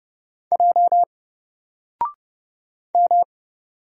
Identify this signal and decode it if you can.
Morse code, heard as a keyed tone, about 30 wpm, 700 Hz. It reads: J M
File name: Single-Letter-Number-30wpm